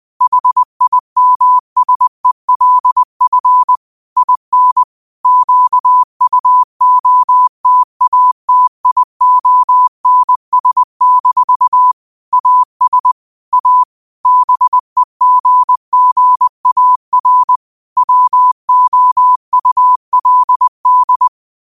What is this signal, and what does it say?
Morse code, heard as a keyed tone, about 20 wpm, 1000 Hz. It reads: HIMSELF IN QUOTATIONS- AS A BEGGAR WOULD
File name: qotdfiles / 20wpm